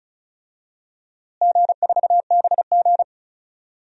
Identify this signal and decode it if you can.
Morse code, heard as a keyed tone, about 36 wpm, 700 Hz. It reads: G4BG
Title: DXCC-100 Call Signs - 4 Characters - Repeated 2x - 35wpm